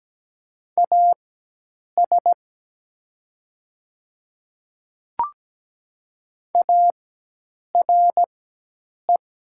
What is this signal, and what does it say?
Morse code, heard as a keyed tone, about 17 wpm, spaced as for 8 wpm, 700 Hz. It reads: AS ARE
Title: 4x Character Spacing - Top 100 Words - 17wpm